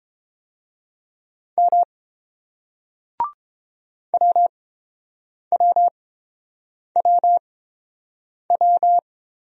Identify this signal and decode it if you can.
Morse code, heard as a keyed tone, about 30 wpm, spaced as for 7 wpm, 700 Hz. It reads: M WWWW